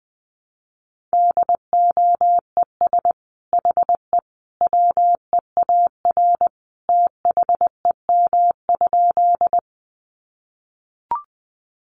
Morse code, heard as a keyed tone, about 20 wpm, 700 Hz. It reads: DOES HE WEAR THEM?
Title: Sentences from Top 1500 Words 20wpm